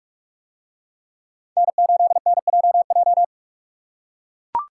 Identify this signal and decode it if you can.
Morse code, heard as a keyed tone, about 44 wpm, 700 Hz. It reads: N8NJJ